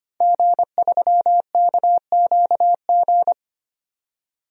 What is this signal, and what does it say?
Morse code, heard as a keyed tone, about 25 wpm, 700 Hz. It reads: G3KQG
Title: DXCC-100 Call Signs - 5 Characters - 25wpm